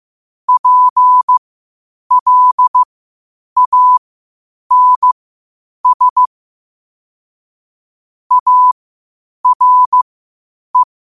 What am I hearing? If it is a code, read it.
Morse code, heard as a keyed tone, about 15 wpm, spaced as for 8 wpm, 1000 Hz. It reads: PLANS ARE
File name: qotdfiles / 8wpm